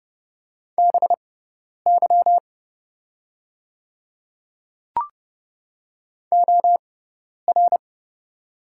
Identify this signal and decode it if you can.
Morse code, heard as a keyed tone, about 30 wpm, spaced as for 10 wpm, 700 Hz. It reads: BY OR
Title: 6x Character Spacing - Top 100 Words - 30wpm